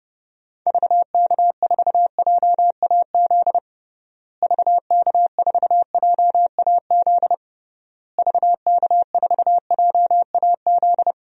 Morse code, heard as a keyed tone, about 30 wpm, 700 Hz. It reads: VK4JAZ VK4JAZ VK4JAZ
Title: DXCC-100 Call Signs - 6 Characters - Repeated 3x - 30wpm